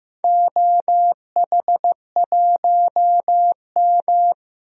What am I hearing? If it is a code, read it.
Morse code, heard as a keyed tone, about 15 wpm, 700 Hz. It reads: OH1M
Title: DXCC-100 Call Signs - 4 Characters - 15wpm